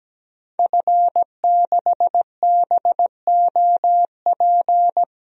Morse code, heard as a keyed tone, about 17 wpm, 700 Hz. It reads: F6BOP
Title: DXCC-100 Call Signs - 5 Characters - 17wpm